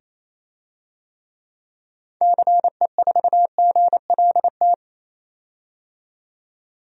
Morse code, heard as a keyed tone, about 28 wpm, 700 Hz. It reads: CE4GLT
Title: DXCC-100 Call Signs - 6 Characters - 28wpm